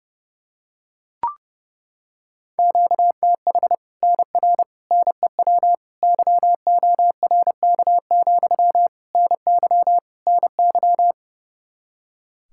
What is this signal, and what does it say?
Morse code, heard as a keyed tone, about 32 wpm, 700 Hz. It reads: QTH NR NEW YORK, NY NY
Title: Practice QSO - Exchange 1 - Beginning - 30wpm